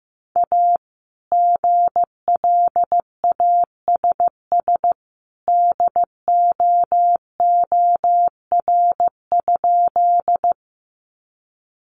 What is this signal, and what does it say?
Morse code, heard as a keyed tone, about 15 wpm, 700 Hz. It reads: A GLASS DOOR?
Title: Sentences from Top 1300 Words 15wpm